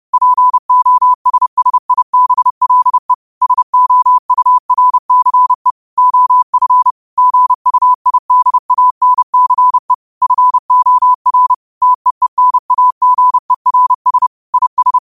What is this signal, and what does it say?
Morse code, heard as a keyed tone, about 30 wpm, 1000 Hz. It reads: POSSIBLE SOURCE OF GUIDANCE FOR TEENAGERS IS
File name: qotdfiles / 30wpm